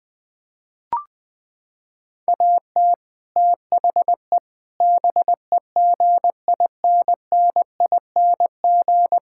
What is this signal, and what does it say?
Morse code, heard as a keyed tone, about 20 wpm, 700 Hz. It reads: AT THE BEGINNING
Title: Sentences with 5-Word Phrases - 20wpm